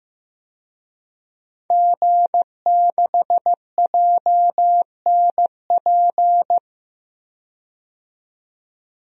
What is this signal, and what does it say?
Morse code, heard as a keyed tone, about 15 wpm, 700 Hz. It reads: G6JNP
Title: DXCC-100 Call Signs - 5 Characters - 15wpm